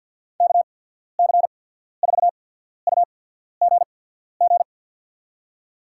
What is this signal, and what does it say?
Morse code, heard as a keyed tone, about 49 wpm, spaced as for 14 wpm, 690 Hz. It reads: KC4UGG
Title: USA Call Signs - 6 Characters - 8x Character Spacing - 50wpm